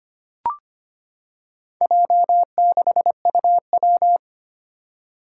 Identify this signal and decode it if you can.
Morse code, heard as a keyed tone, about 25 wpm, 700 Hz. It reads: J6UW
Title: Four Challenging Letter-Number Combinations 25wpm